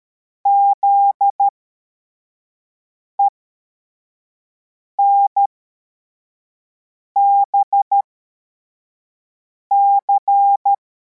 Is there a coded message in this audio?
Yes: Morse code from 0.5 to 10.7 s, heard as a keyed tone, about 13 wpm, spaced as for 4 wpm, 800 Hz. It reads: ZENBC